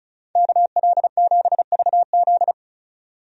Morse code, heard as a keyed tone, about 36 wpm, 700 Hz. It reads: KL7VZ
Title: DXCC-100 Call Signs - 5 Characters - 35wpm